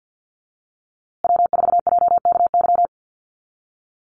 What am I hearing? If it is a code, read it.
Morse code, heard as a keyed tone, about 50 wpm, 700 Hz. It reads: W4JKY
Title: USA Call Signs Repeated 2x 50wpm